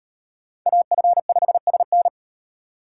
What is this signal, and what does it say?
Morse code, heard as a keyed tone, about 36 wpm, 700 Hz. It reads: AF5SN